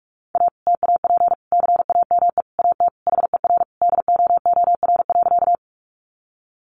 Morse code, heard as a keyed tone, about 44 wpm, 700 Hz. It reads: A TAP CAME AT HER DOOR.
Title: Sentences from Top 1400 Words 45wpm